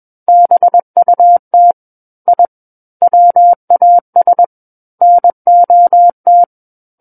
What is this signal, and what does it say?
Morse code, heard as a keyed tone, about 21 wpm, 700 Hz. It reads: BUT I WAS NOT
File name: Book One / 12-Chapter 12 - 21 WPM